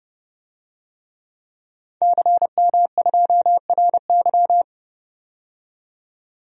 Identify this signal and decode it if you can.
Morse code, heard as a keyed tone, about 30 wpm, 700 Hz. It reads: CM2RY
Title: DXCC-100 Call Signs - 5 Characters - 30wpm